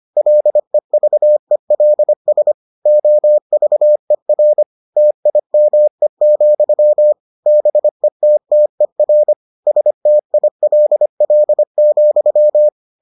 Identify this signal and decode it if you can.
Morse code, heard as a keyed tone, about 25 wpm, 600 Hz. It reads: LEVELS OVER TIME, BETTER STILL,